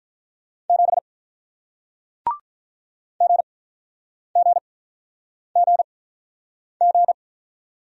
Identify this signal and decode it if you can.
Morse code, heard as a keyed tone, about 46 wpm, spaced as for 9 wpm, 700 Hz. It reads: 7 GGGG